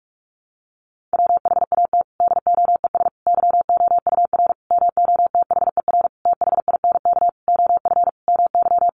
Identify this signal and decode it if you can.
Morse code, heard as a keyed tone, about 44 wpm, 700 Hz. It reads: WHAT DOES YOUR MOTHER THINK OF MY
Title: Sentences from Top 1400 Words 45wpm